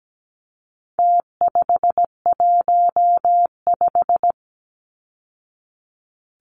Morse code, heard as a keyed tone, about 17 wpm, 700 Hz. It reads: T515
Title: Numbers - Contest Style - 4-Digit 17wpm